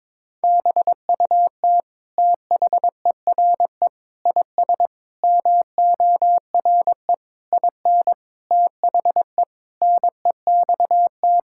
Morse code, heard as a keyed tone, about 21 wpm, 700 Hz. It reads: BUT THERE IS MORE IN THE NEXT
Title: Sentences from Top 1000 Words - Review Part 5 - 22wpm